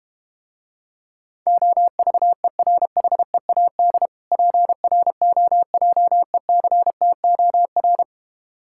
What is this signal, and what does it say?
Morse code, heard as a keyed tone, about 32 wpm, 700 Hz. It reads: OVERHEAD PROJECTOR